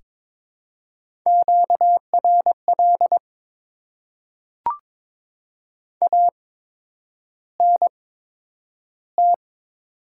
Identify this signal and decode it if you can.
Morse code, heard as a keyed tone, about 22 wpm, spaced as for 9 wpm, 700 Hz. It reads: QRL ANT
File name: Top 20 QSO Elements - 8x Character Spacing - 22wpm